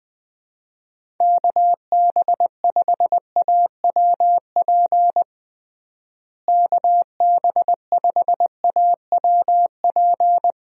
Morse code, heard as a keyed tone, about 20 wpm, 700 Hz. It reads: KB5AWP KB5AWP
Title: USA Call Signs - 6 Characters - Repeated 2x - 20wpm